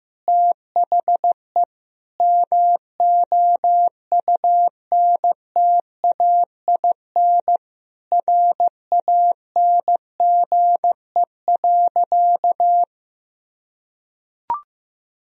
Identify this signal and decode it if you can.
Morse code, heard as a keyed tone, about 16 wpm, 700 Hz. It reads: THE MOUNTAIN RANGE.
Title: Sentences from Top 1000 Words - Review Part 4 - 15wpm